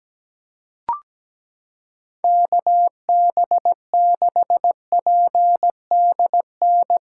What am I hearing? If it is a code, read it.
Morse code, heard as a keyed tone, about 17 wpm, 700 Hz. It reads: KB6PDN
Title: USA Callsign DE Callsign - 6 Characters - 17wpm